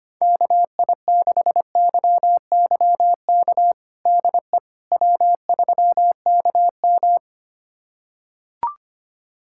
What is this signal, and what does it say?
Morse code, heard as a keyed tone, about 25 wpm, 700 Hz. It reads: KI6YYK DE W3KM
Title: USA Callsign DE Callsign 25wpm